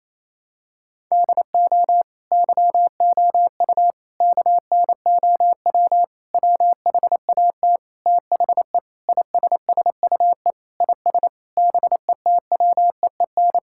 Morse code, heard as a keyed tone, about 27 wpm, 700 Hz. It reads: DO YOU KNOW WHAT THE ISSUE IS BETWEEN
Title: Sentences from Top 1000 Words - Review Part 6 - 28wpm